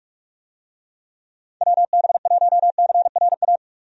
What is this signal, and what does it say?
Morse code, heard as a keyed tone, about 46 wpm, 700 Hz. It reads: WB1CRA
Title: USA Call Signs - 6 Characters - 45wpm